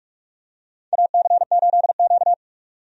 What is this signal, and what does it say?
Morse code, heard as a keyed tone, about 45 wpm, 700 Hz. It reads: AC8Q
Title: USA Call Signs - 4 Characters - 45wpm